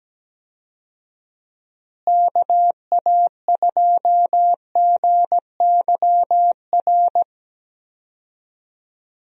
Morse code, heard as a keyed tone, about 17 wpm, 700 Hz. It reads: KA2GYR
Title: USA Call Signs - 6 Characters - Repeated 3x - 17wpm